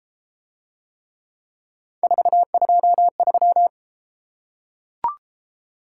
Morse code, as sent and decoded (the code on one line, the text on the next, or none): ....- ..--- ...--
423